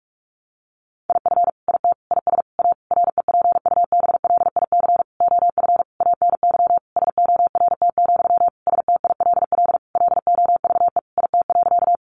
Morse code, heard as a keyed tone, about 44 wpm, 700 Hz. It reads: IF IT IS A REPUBLIC OF ANY SORT, STILL LOVE IT.
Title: Sentences from Top 1500 Words 45wpm